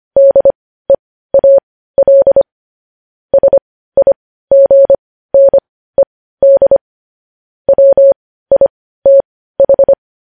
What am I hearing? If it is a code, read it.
Morse code, heard as a keyed tone, about 25 wpm, spaced as for 15 wpm, 550 Hz. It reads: DEAL SIGNED WITH
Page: news headlines in morse code at 15 wpm